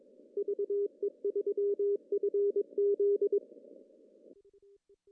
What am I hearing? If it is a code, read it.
Morse code, heard as a keyed tone, about 22 wpm, 400 Hz. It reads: VE3FZ
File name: CW / cqww-cw-k5tr